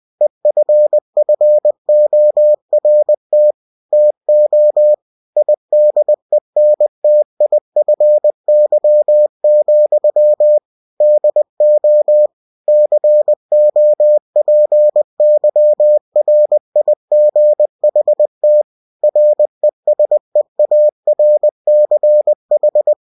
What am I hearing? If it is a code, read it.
Morse code, heard as a keyed tone, about 20 wpm, 600 Hz. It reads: EFFORT TO IDENTIFY, DO COPYRIGHT RESEARCH